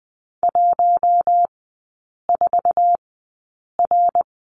Morse code, heard as a keyed tone, about 20 wpm, spaced as for 8 wpm, 700 Hz. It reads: 14R
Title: 067 Lesson 12e - Letter d - Three Characters 20wpm